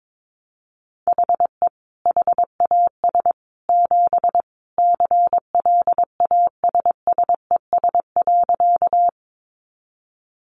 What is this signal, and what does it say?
Morse code, heard as a keyed tone, about 22 wpm, 700 Hz. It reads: HE HAS 7 CLASSES.
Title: Additional Easy Phrases 22wpm